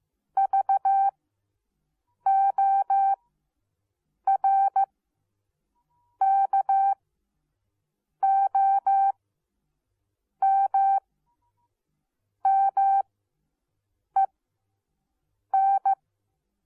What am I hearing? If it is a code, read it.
Morse code, heard as a keyed tone, about 15 wpm, spaced as for 6 wpm, 780 Hz. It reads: VORKOMMEN